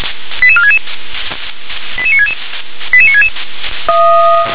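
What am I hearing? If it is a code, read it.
Five-tone selective call (ZVEI), heard as quick runs of five tones, tones of about 70 ms. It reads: 88466 88966 88966
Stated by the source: ZVEI five-tone selective call received on an SDR.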